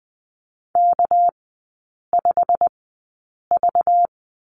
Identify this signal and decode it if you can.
Morse code, heard as a keyed tone, about 20 wpm, spaced as for 8 wpm, 700 Hz. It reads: K5V
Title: 198 Lesson 32e - Letter k - Three Characters 20wpm